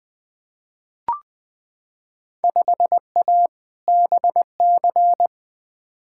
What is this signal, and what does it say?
Morse code, heard as a keyed tone, about 20 wpm, 700 Hz. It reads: 5A BC
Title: Field Day Exchanges - 20wpm